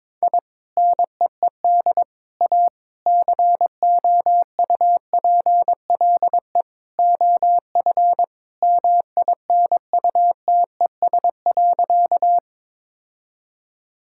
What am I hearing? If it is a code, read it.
Morse code, heard as a keyed tone, about 21 wpm, 700 Hz. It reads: I NEED A COUPLE OF MINUTES.
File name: practice / 22wpm / Sentences with 3-Word Phrases - 22wpm